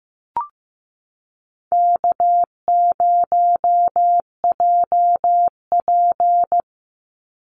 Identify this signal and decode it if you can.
Morse code, heard as a keyed tone, about 14 wpm, 700 Hz. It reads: K0JP
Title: Easy Call Signs - 15wpm